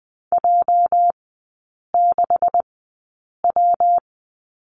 Morse code, heard as a keyed tone, about 20 wpm, spaced as for 8 wpm, 700 Hz. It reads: J6W
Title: 205 Lesson 33e - Letter j - Three Characters 20wpm